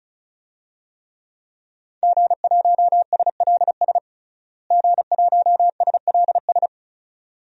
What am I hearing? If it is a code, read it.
Morse code, heard as a keyed tone, about 36 wpm, 700 Hz. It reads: G1SLS G1SLS
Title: DXCC-100 Call Signs - 5 Characters - Repeated 2x - 35wpm